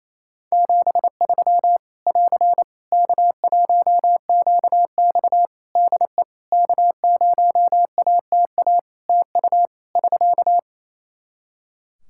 Morse code, heard as a keyed tone, about 28 wpm, 700 Hz. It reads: 73 <AR> K1QX DE K0ATA TU <SK>
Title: Practice QSO - Exchange 4 - Ending - 28wpm